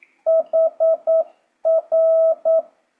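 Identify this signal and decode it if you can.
Morse code, heard as a keyed tone, about 9 wpm, 650 Hz. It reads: HR
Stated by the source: Morse code speed-up practice at 9 WPM, from JH8CBH.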